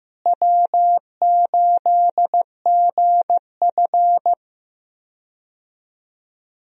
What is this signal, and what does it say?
Morse code, heard as a keyed tone, about 16 wpm, 700 Hz. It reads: W8GF
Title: USA Call Signs - 4 Characters - Repeated 2x - 15wpm